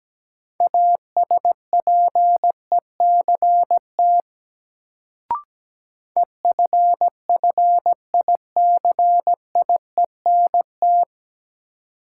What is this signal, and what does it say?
Morse code, heard as a keyed tone, about 17 wpm, 700 Hz. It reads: ASPECT EFFICIENT
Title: Sentences from Top 1300 Words - New Word Drill Alt - 17wpm